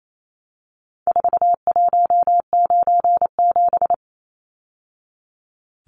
Morse code, heard as a keyed tone, about 28 wpm, 700 Hz. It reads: 4197
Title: Numbers - 4-Digit 28wpm